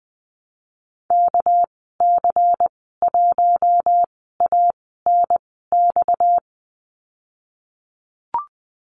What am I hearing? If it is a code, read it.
Morse code, heard as a keyed tone, about 20 wpm, spaced as for 14 wpm, 700 Hz. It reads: KC1ANX
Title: 238 Lesson 38g - Letter q - Call Signs 20wpm